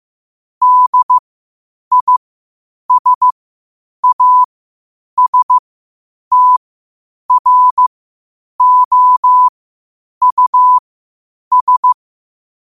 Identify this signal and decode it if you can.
Morse code, heard as a keyed tone, about 15 wpm, spaced as for 8 wpm, 1000 Hz. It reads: DISASTROUS
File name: qotdfiles / 8wpm